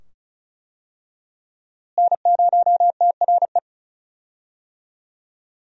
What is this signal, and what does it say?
Morse code, heard as a keyed tone, about 36 wpm, 700 Hz. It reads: N0TRE
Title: USA Call Signs - 5 Characters - Repeated 2x - 35wpm